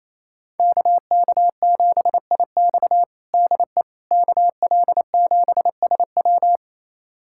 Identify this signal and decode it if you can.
Morse code, heard as a keyed tone, about 28 wpm, 700 Hz. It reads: KK7IX DE KL7SW
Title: USA Callsign DE Callsign - 5 Characters - 28wpm